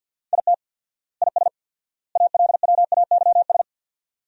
Morse code, heard as a keyed tone, about 45 wpm, 700 Hz. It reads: IT IS ALWAYS
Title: 4x Word-Spacing - Sentences from Top 300 Words - 50wpm